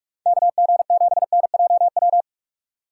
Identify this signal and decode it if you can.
Morse code, heard as a keyed tone, about 44 wpm, 700 Hz. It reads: KG7NJW
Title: USA Call Signs - 6 Characters - 6x Character Spacing - 45wpm